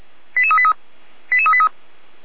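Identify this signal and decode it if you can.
Five-tone selective call (ZVEI), heard as quick runs of five tones, tones of about 70 ms. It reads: 88383 88383
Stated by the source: Five-tone ZVEI selective call 88405 received by SDR, 455 channel.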